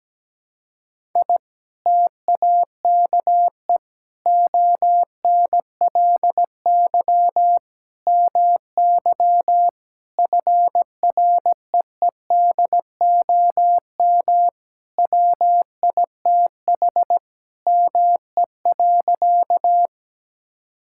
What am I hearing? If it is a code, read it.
Morse code, heard as a keyed tone, about 17 wpm, 700 Hz. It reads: I TAKE ONLY MY FREEDOM WITH ME.
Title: Sentences from Top 1000 Words - Review Part 5 - 17wpm